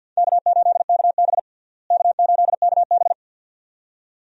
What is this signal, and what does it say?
Morse code, heard as a keyed tone, about 53 wpm, 700 Hz. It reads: K8KB K8KB
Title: USA Call Signs - 4 Characters - Repeated 2x - 50wpm